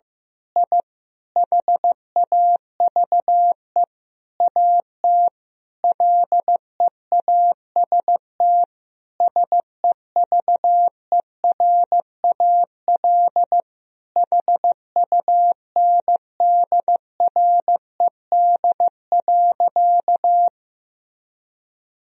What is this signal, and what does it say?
Morse code, heard as a keyed tone, about 15 wpm, 700 Hz. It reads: I HAVE AT LEAST SEVERAL HUNDRED.
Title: Sentences from Top 1000 Words - Review Part 2 - 15wpm